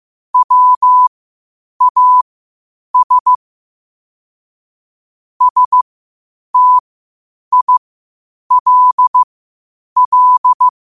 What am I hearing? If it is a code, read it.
Morse code, heard as a keyed tone, about 15 wpm, spaced as for 8 wpm, 1000 Hz. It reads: WAS STILL
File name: qotdfiles / 8wpm